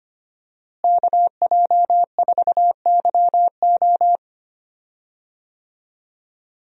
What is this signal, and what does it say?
Morse code, heard as a keyed tone, about 25 wpm, 700 Hz. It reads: KJ4YO
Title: USA Call Signs - 5 Characters - Repeated 2x - 25wpm